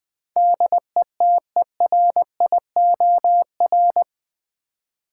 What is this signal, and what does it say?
Morse code, heard as a keyed tone, about 20 wpm, 700 Hz. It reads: DETERIOR
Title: Build A Word - 4 Letters At A Time - 11-Letter Words - 20wpm